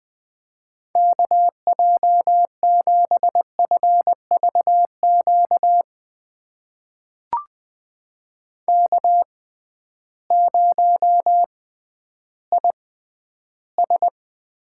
Morse code, heard as a keyed tone, about 20 wpm, spaced as for 12 wpm, 700 Hz. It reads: KJ7FVQ K0IS
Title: USA Call Signs - 6x Character Spacing - 20wpm